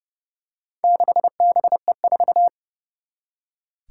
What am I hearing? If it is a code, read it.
Morse code, heard as a keyed tone, about 31 wpm, 700 Hz. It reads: 6BE4
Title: Four Challenging Letter-Number Combinations 30wpm